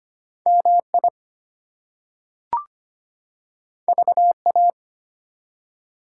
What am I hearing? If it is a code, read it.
Morse code, heard as a keyed tone, about 25 wpm, 700 Hz. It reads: MI VA
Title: US States and Canadian Provinces 25wpm